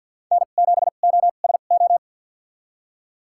Morse code, heard as a keyed tone, about 52 wpm, spaced as for 36 wpm, 700 Hz. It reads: N7OSO
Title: USA Call Signs - 5 Characters - 2x Character Spacing - 50wpm